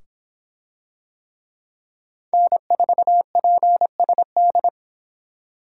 Morse code, heard as a keyed tone, about 24 wpm, 700 Hz. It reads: N4PSD